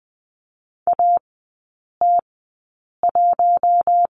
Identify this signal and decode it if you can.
Morse code, heard as a keyed tone, about 20 wpm, spaced as for 8 wpm, 700 Hz. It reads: AT1
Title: 159 Lesson 26e - Letter g - Three Characters 20wpm